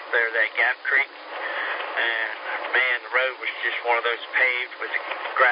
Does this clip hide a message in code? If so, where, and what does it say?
Yes: Emergency Alert System (SAME) tones (header bursts) from 0.5 to 5.1 s, heard as harsh buzzing bursts, with speech.